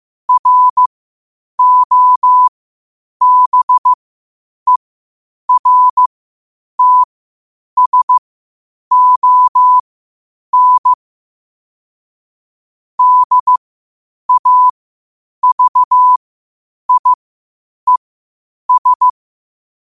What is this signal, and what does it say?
Morse code, heard as a keyed tone, about 15 wpm, spaced as for 8 wpm, 1000 Hz. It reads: ROBERTSON DAVIES
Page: qotd-at-8-wpm